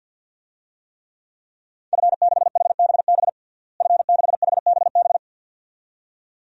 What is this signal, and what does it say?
Morse code, heard as a keyed tone, about 52 wpm, 700 Hz. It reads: F6HBB F6HBB
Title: DXCC-100 Call Signs - 5 Characters - Repeated 2x - 50wpm